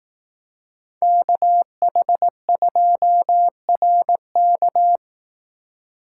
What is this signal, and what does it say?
Morse code, heard as a keyed tone, about 18 wpm, 700 Hz. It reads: KH2RK
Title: Speed Racing - USA 2x2 Call Signs 15wpm